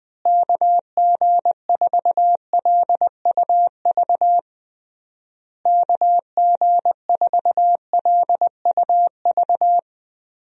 Morse code, heard as a keyed tone, about 21 wpm, 700 Hz. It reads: KG4LUV KG4LUV
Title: USA Call Signs - 6 Characters - Repeated 3x - 20wpm